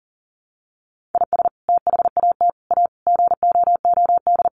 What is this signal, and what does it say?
Morse code, heard as a keyed tone, about 40 wpm, 700 Hz. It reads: IS THAT A GOOD
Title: Sentences from Top 1500 Words 40wpm